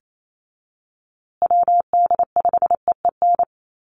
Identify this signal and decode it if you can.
Morse code, heard as a keyed tone, about 28 wpm, 700 Hz. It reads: WD5EEN
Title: USA Call Signs Repeated 3x 28wpm